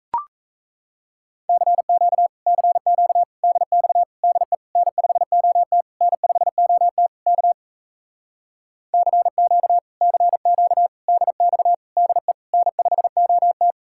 Morse code, heard as a keyed tone, about 38 wpm, 700 Hz. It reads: CQ CQ DX DE N5OT N5OT K CQ CQ DX DE N5OT